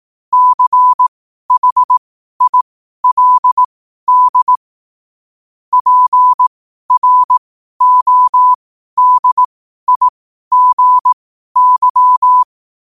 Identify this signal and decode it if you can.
Morse code, heard as a keyed tone, about 18 wpm, spaced as for 12 wpm, 1000 Hz. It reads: CHILD PRODIGY
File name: qotdfiles / 12wpm